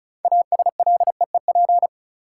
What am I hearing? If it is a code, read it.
Morse code, heard as a keyed tone, about 36 wpm, 700 Hz. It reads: ASLEEP
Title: Six Letter Words - 35wpm